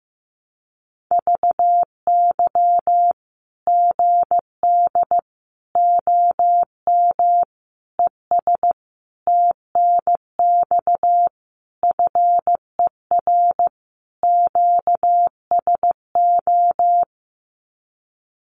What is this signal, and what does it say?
Morse code, heard as a keyed tone, about 14 wpm, 700 Hz. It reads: VY GD OM ES TNX FER QSO
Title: Common QSO Phrases 15wpm